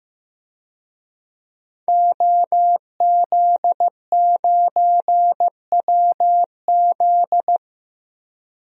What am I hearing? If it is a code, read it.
Morse code, heard as a keyed tone, about 15 wpm, 700 Hz. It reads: OZ9WZ
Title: DXCC-100 Call Signs - 5 Characters - 15wpm